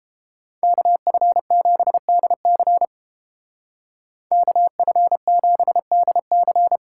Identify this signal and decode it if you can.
Morse code, heard as a keyed tone, about 29 wpm, 700 Hz. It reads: KF7DC KF7DC